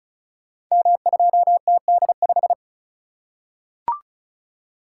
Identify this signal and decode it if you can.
Morse code, heard as a keyed tone, about 36 wpm, 700 Hz. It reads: M2TD5
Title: Code Groups - 5 Letter:Numbers - 35wpm